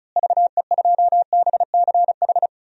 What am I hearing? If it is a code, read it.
Morse code, heard as a keyed tone, about 36 wpm, 700 Hz. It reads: VE2BCH
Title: DXCC-100 Call Signs - 6 Characters - Repeated 2x - 35wpm